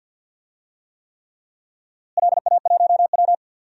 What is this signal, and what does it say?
Morse code, heard as a keyed tone, about 47 wpm, 700 Hz. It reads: LA1W
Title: DXCC-100 Call Signs - 4 Characters - Repeated 3x - 50wpm